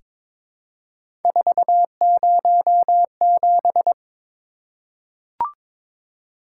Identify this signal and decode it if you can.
Morse code, heard as a keyed tone, about 22 wpm, 700 Hz. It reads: 407
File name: practice / 22wpm / Three Letter-Number Combinations - 22wpm